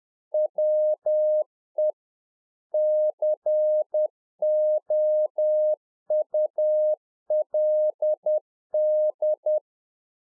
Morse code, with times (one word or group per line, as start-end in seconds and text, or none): 0.3-1.9 WE
2.7-9.6 COULD